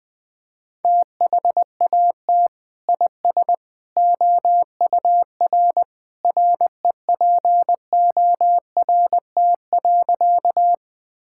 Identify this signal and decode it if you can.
Morse code, heard as a keyed tone, about 20 wpm, 700 Hz. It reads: THAT IS OUR REPORT.
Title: Sentences from Top 1000 Words - Review Part 6 - 20wpm